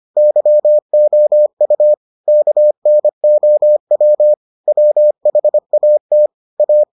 Morse code, heard as a keyed tone, about 25 wpm, 600 Hz. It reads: YOU KNOW WHAT A